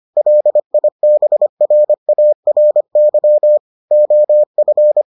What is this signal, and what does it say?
Morse code, heard as a keyed tone, about 25 wpm, 600 Hz. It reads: LIBRARY OF